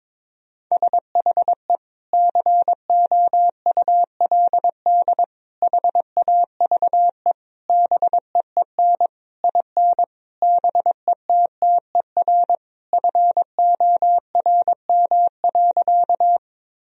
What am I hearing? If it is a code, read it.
Morse code, heard as a keyed tone, about 21 wpm, 700 Hz. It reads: SHE COULD HAVE BEEN IN BETTER FORM.
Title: Sentences from Top 1000 Words - Review Part 1 - 22wpm